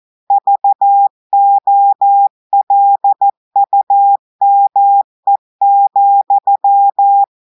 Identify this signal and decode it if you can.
Morse code, heard as a keyed tone, about 14 wpm, 800 Hz. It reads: VOLUME,
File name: audios / portugais / portugais texte en clair 1 vitesse 14 mots minutes